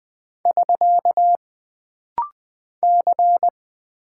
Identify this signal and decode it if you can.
Morse code, heard as a keyed tone, about 20 wpm, 700 Hz. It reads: <SK> C
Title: Single Character (Letter:Number:Punctuation:Prosign) - Rapid-Fire - 20wpm